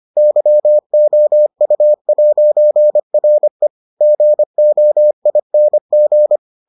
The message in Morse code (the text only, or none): YOU'RE GOING